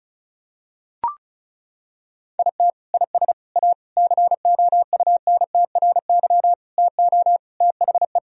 Morse code, heard as a keyed tone, about 36 wpm, 700 Hz. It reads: IT IS A COUNTRY TO THE
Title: Sentences from Top 1000 Words - Review Part 1 - 35wpm